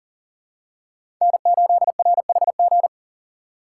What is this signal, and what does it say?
Morse code, heard as a keyed tone, about 39 wpm, 700 Hz. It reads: N8RHG
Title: USA Call Signs - 5 Characters - 6x Character Spacing - 40wpm